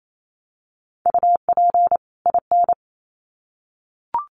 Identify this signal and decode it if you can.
Morse code, heard as a keyed tone, about 28 wpm, 700 Hz. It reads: UP IN
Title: Sets of 2 Words 28wpm